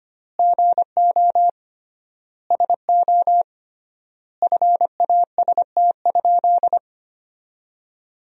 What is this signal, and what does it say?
Morse code, heard as a keyed tone, about 25 wpm, 700 Hz. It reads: GO SO FAST?
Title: 3x Word-Spacing - Sentences from Top 500 Words - 25wpm